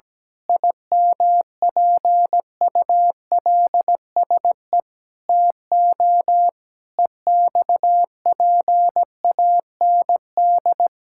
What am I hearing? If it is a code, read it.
Morse code, heard as a keyed tone, about 17 wpm, 700 Hz. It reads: IMPULSE TO EXPAND
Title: Pangram Challenge v2 - 17wpm